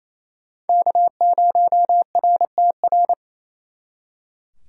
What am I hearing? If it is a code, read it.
Morse code, heard as a keyed tone, about 34 wpm, 700 Hz. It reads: K0RTR
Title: Speed Racing - USA 1x3 Call Signs 28wpm